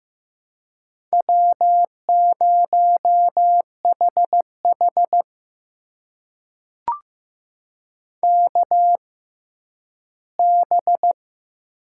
Morse code, heard as a keyed tone, about 15 wpm, spaced as for 10 wpm, 700 Hz. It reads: W0HH KB
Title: USA Call Signs - 6x Character Spacing - 15wpm